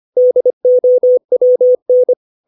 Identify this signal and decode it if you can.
Morse code, heard as a keyed tone, about 25 wpm, 500 Hz. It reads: DOWN